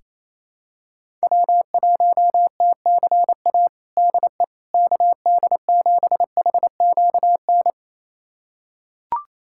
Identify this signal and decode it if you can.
Morse code, heard as a keyed tone, about 27 wpm, 700 Hz. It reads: W1TCA DE KD7HQN
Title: USA Callsign DE Callsign 28wpm